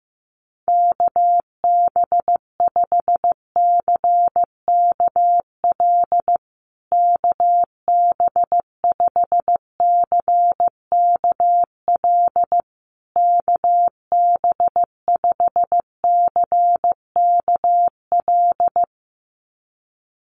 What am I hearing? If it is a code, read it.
Morse code, heard as a keyed tone, about 15 wpm, 700 Hz. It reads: KB5CKL KB5CKL KB5CKL
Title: USA Call Signs Repeated 3x 15wpm